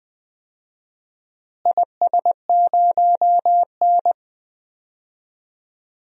Morse code, heard as a keyed tone, about 20 wpm, 700 Hz. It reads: IS0N
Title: DXCC-100 Call Signs - 4 Characters - Repeated 2x - 20wpm